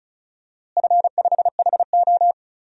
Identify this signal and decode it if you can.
Morse code, heard as a keyed tone, about 36 wpm, 700 Hz. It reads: F5HO
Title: DXCC-100 Call Signs - 4 Characters - Repeated 3x - 35wpm